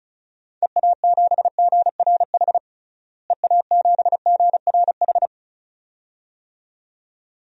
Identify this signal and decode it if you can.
Morse code, heard as a keyed tone, about 36 wpm, 700 Hz. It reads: EA7GRH EA7GRH
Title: DXCC-100 Call Signs - 6 Characters - Repeated 2x - 35wpm